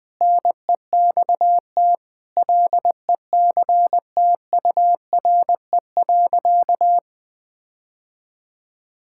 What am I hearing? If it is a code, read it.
Morse code, heard as a keyed tone, about 20 wpm, 700 Hz. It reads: NEXT LECTURE.
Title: Sentences from Top 1600 Words 20wpm